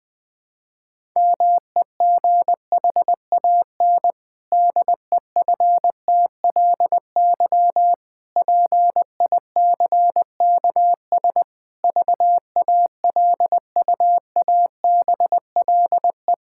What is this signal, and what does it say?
Morse code, heard as a keyed tone, about 20 wpm, 700 Hz. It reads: MEGHAN DEFTLY PICKS VALUABLE
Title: Pangram Challenge v2 - 20wpm